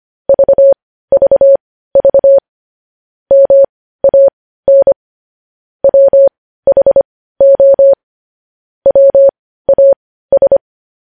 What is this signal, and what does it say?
Morse code, heard as a keyed tone, about 25 wpm, spaced as for 15 wpm, 550 Hz. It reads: VVV MAN WHO WAS
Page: news headlines in morse code at 15 wpm